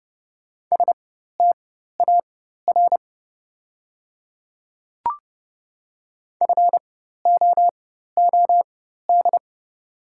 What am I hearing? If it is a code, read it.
Morse code, heard as a keyed tone, about 29 wpm, spaced as for 14 wpm, 690 Hz. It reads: STAR FOOD
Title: 4x Character Spacing - Top 500 Words - 30wpm